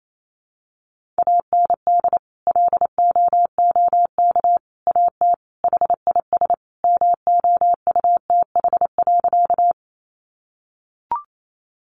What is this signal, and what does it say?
Morse code, heard as a keyed tone, about 28 wpm, 700 Hz. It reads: AND LOOK AT HIS MOUTH.
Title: Sentences from Top 1500 Words 28wpm